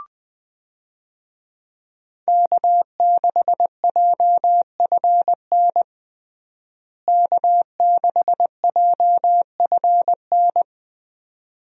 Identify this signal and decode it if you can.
Morse code, heard as a keyed tone, about 20 wpm, 700 Hz. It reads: K6JFN K6JFN
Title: USA Call Signs - 5 Characters - Repeated 2x - 20wpm